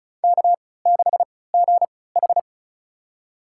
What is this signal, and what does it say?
Morse code, heard as a keyed tone, about 36 wpm, spaced as for 20 wpm, 700 Hz. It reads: K6GH
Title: USA Call Signs - 4 Characters - 3x Character Spacing - 35wpm